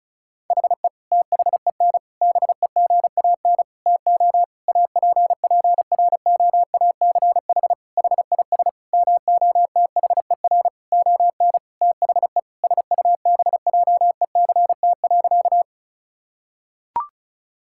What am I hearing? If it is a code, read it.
Morse code, heard as a keyed tone, about 36 wpm, 700 Hz. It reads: HE THEN BEGAN TO APPROACH HIS MOTHER ON THE SUBJECT.
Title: Sentences from Top 1000 Words - Review Part 6 - 35wpm